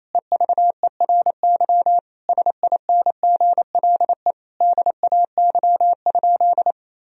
Morse code, heard as a keyed tone, about 27 wpm, 700 Hz. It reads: EVERY SINGLE DAY?
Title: Sentences from Top 1000 Words - Review Part 4 - 28wpm